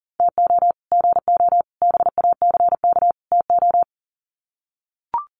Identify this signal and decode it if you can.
Morse code, heard as a keyed tone, about 40 wpm, 700 Hz. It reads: TO GO BACK TO
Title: Sets of 4 Words 40wpm